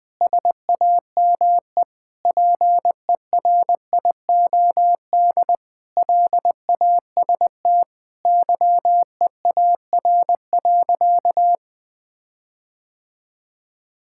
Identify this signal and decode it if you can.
Morse code, heard as a keyed tone, about 21 wpm, 700 Hz. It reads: SAME PERIOD LAST YEAR.
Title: Sentences with 7-Word Phrases - Spoken Beginning - 20wpm